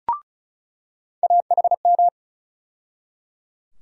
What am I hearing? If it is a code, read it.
Morse code, heard as a keyed tone, about 34 wpm, 700 Hz. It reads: AHM
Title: USA Call Sign Postfixes - 3 Letters 35wpm